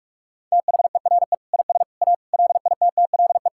Morse code, heard as a keyed tone, about 46 wpm, 700 Hz. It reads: THERE IS A LITTLE
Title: Sentences from Top 1000 Words - Review Part 5 - 45wpm